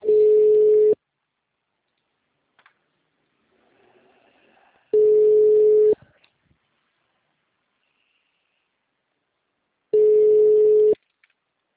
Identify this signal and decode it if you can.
Ringback tone (Europe, 425 Hz).